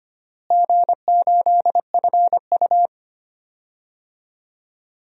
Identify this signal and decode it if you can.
Morse code, heard as a keyed tone, about 25 wpm, 700 Hz. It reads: G8FU
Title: DXCC-100 Call Signs - 4 Characters - Repeated 2x - 25wpm